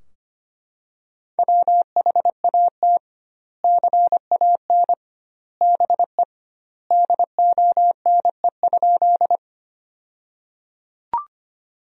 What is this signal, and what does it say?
Morse code, heard as a keyed tone, about 25 wpm, 700 Hz. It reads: WHAT CAN BE DONE?
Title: 2x Word-Spacing - Sentences from Top 500 Words - 25wpm